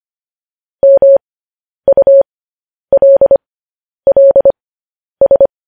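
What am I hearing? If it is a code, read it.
Morse code, heard as a keyed tone, about 25 wpm, spaced as for 10 wpm, 550 Hz. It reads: MULLS
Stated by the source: Morse code news headlines practice transmission.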